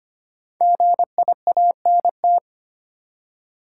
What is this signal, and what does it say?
Morse code, heard as a keyed tone, about 25 wpm, 700 Hz. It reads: GIANT
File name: 5 Letter Words - First 3 Letters Given - 25wpm